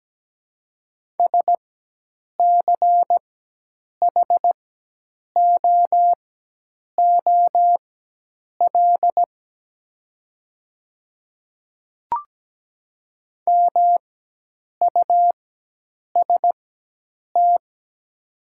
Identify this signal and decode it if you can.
Morse code, heard as a keyed tone, about 17 wpm, spaced as for 8 wpm, 700 Hz. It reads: SCHOOL MUST
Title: 4x Character Spacing - Top 300 Words - 17wpm